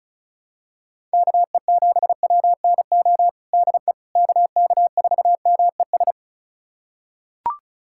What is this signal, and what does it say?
Morse code, heard as a keyed tone, about 36 wpm, 700 Hz. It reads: KE7WNO DE KK4MES
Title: USA Callsign DE Callsign - 6 Characters - 35wpm